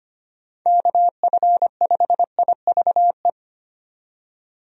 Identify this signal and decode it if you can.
Morse code, heard as a keyed tone, about 25 wpm, 700 Hz. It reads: KF5IVE
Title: USA Call Signs - 6 Characters - 25wpm